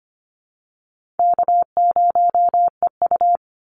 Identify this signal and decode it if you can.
Morse code, heard as a keyed tone, about 25 wpm, 700 Hz. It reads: K0EU
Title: Call Signs Repeated 3x - 25wpm